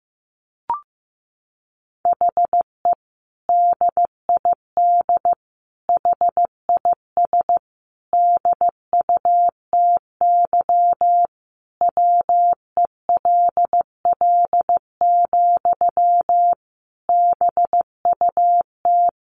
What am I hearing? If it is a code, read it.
Morse code, heard as a keyed tone, about 15 wpm, 700 Hz. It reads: HE DID HIS DUTY WELL, BUT
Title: Sentences from Top 1500 Words 15wpm